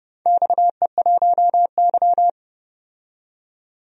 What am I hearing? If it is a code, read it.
Morse code, heard as a keyed tone, about 30 wpm, 700 Hz. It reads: XE1Y
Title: DXCC-100 Call Signs - 4 Characters - 30wpm